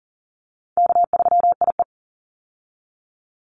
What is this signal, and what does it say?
Morse code, heard as a keyed tone, about 41 wpm, 700 Hz. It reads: K3IE
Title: Call Signs - 40wpm